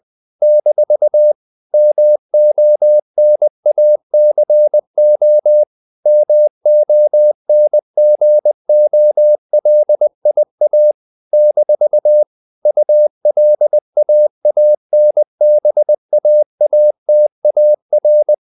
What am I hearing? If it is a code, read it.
Morse code, heard as a keyed tone, about 20 wpm, 600 Hz. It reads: - MONACO MONGOLIA - ULAANBAATAR
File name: CW / Practice / NationalCapitals20